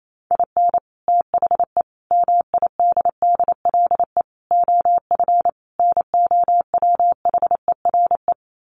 Morse code, heard as a keyed tone, about 28 wpm, 700 Hz. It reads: IN THE MIDDLE OF NOWHERE
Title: Sets of 5 Words Encore 28wpm